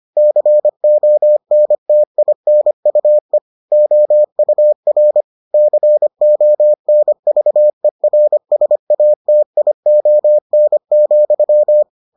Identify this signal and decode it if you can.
Morse code, heard as a keyed tone, about 25 wpm, 600 Hz. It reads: CONTINUE OUR CONVERSATION,